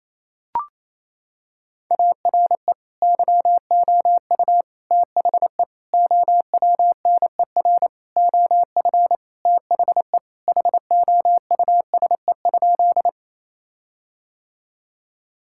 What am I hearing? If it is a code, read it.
Morse code, heard as a keyed tone, about 28 wpm, 700 Hz. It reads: ARE YOU THE OWNER OF THE HOUSE?
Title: Sentences from Top 1600 Words 28wpm